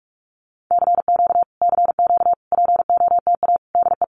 Morse code, heard as a keyed tone, about 46 wpm, 700 Hz. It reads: CQ CQ POTA DE
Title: Common QSO Phrases Encore 45wpm